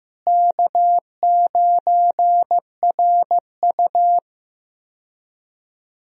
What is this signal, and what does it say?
Morse code, heard as a keyed tone, about 15 wpm, 700 Hz. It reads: K9RU
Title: USA Call Signs - 4 Characters - Repeated 2x - 15wpm